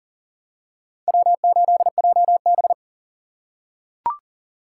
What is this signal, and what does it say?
Morse code, heard as a keyed tone, about 40 wpm, 700 Hz. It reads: W8JB
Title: USA Call Signs - 4 Characters - 4x Character Spacing - 40wpm